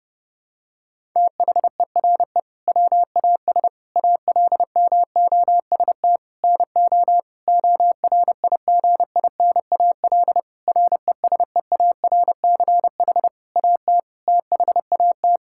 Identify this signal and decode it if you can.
Morse code, heard as a keyed tone, about 32 wpm, 700 Hz. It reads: THERE WAS ALMOST NO ORIGINAL RESEARCH AT THAT
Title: Sentences from Top 1000 Words - Review Part 6 - 30wpm